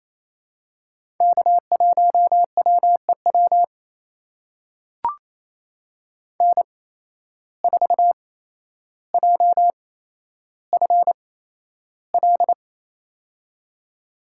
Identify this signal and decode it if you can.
Morse code, heard as a keyed tone, about 28 wpm, spaced as for 12 wpm, 700 Hz. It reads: K1WEW N4JFL
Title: USA Call Signs - 5 Characters - 8x Character Spacing - 28wpm